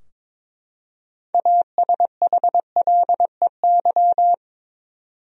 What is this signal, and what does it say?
Morse code, heard as a keyed tone, about 22 wpm, 700 Hz. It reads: ASHLEY
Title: USA Common Names - 22wpm